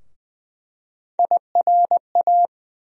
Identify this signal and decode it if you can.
Morse code, heard as a keyed tone, about 20 wpm, 700 Hz. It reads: IRA
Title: USA Call Sign Postfixes - 3 Letters 20wpm